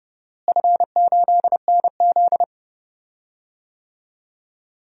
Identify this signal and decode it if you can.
Morse code, heard as a keyed tone, about 32 wpm, 700 Hz. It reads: F8NZ
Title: DXCC-100 Call Signs - 4 Characters - Repeated 2x - 30wpm